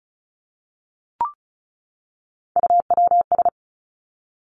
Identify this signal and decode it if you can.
Morse code, heard as a keyed tone, about 36 wpm, 700 Hz. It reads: UWS
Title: Three Challenging Letter-Number Combinations 35wpm